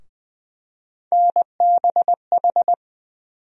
Morse code, heard as a keyed tone, about 20 wpm, 700 Hz. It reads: NBH
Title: USA Call Sign Postfixes - 3 Letters 20wpm